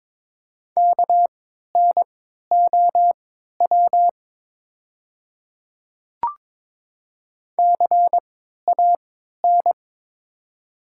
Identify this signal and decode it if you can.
Morse code, heard as a keyed tone, about 22 wpm, spaced as for 12 wpm, 700 Hz. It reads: KNOW CAN
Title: 3x Character Spacing - Top 100 Words - 22wpm